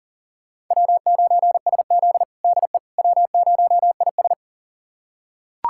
Morse code, heard as a keyed tone, about 40 wpm, 700 Hz. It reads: W9SZ DE W0IS
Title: USA Callsign DE Callsign - 4 Characters - 40wpm